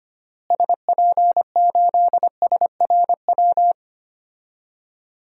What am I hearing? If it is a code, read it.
Morse code, heard as a keyed tone, about 25 wpm, 700 Hz. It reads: SP8SRW
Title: DXCC-100 Call Signs - 6 Characters - Repeated 3x - 25wpm